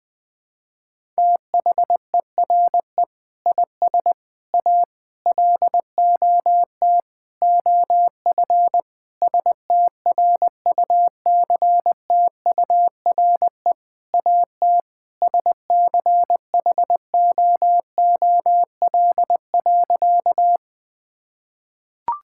Morse code, heard as a keyed tone, about 20 wpm, 700 Hz. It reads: THERE IS A LOT OF STRUCTURE AT SCHOOL.
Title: Sentences from Top 1000 Words - Review Part 6 - 20wpm